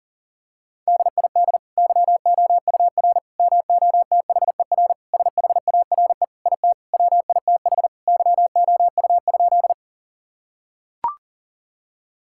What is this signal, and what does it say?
Morse code, heard as a keyed tone, about 40 wpm, 700 Hz. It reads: DID YOUR MOTHER SHARE IT WITH YOU?
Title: Sentences from Top 1000 Words - Review Part 6 - 40wpm